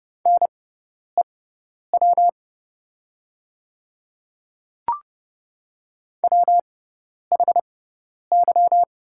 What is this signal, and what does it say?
Morse code, heard as a keyed tone, about 30 wpm, spaced as for 10 wpm, 700 Hz. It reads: NEW WHY
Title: 6x Character Spacing - Top 300 Words - 30wpm